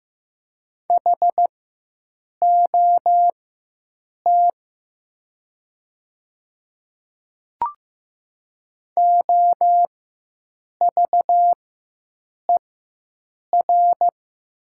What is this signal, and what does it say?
Morse code, heard as a keyed tone, about 15 wpm, spaced as for 7 wpm, 700 Hz. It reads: HOT OVER
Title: 4x Character Spacing - Top 100 Words - 15wpm